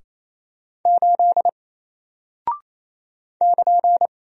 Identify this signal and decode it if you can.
Morse code, heard as a keyed tone, about 28 wpm, 700 Hz. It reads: 8 <KN>
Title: Single Character (Letter:Number:Punctuation:Prosign) Alt 28wpm